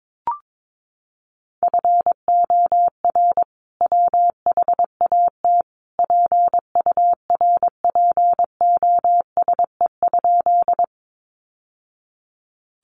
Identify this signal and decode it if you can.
Morse code, heard as a keyed tone, about 22 wpm, 700 Hz. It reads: FOR WHAT PURPOSE?
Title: Sentences from Top 800 Words 22wpm